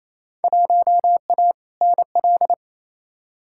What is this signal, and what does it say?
Morse code, heard as a keyed tone, about 28 wpm, 700 Hz. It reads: 1A NL
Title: Field Day Exchanges Encore - 28wpm